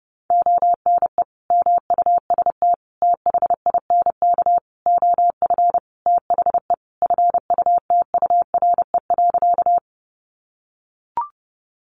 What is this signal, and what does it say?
Morse code, heard as a keyed tone, about 30 wpm, 700 Hz. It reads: ONE MUST THINK OF THE FUTURE.
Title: Sentences from Top 700 Words 30wpm